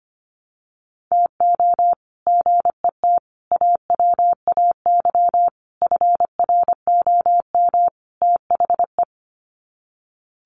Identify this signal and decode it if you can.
Morse code, heard as a keyed tone, about 25 wpm, 700 Hz. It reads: TO GET AWAY FROM THE
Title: Sets of 5 Words Encore2 25wpm